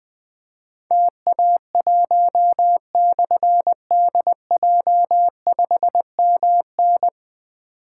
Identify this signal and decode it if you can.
Morse code, heard as a keyed tone, about 20 wpm, 700 Hz. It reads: TA1/DJ5MN
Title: Difficult Call Signs 20wpm